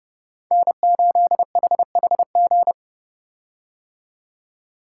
Morse code, heard as a keyed tone, about 30 wpm, 700 Hz. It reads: N8HHG
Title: K1USN SST QSO Practice - 30wpm